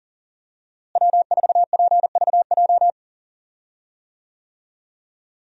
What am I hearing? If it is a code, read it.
Morse code, heard as a keyed tone, about 40 wpm, 700 Hz. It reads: W4PVJ
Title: USA Call Signs - 5 Characters - Repeated 2x - 40wpm